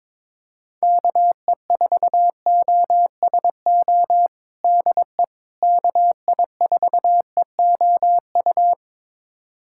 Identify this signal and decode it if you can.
Morse code, heard as a keyed tone, about 22 wpm, 700 Hz. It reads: KE4OSO DE KI4EOU
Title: USA Callsign DE Callsign - 6 Characters - 22wpm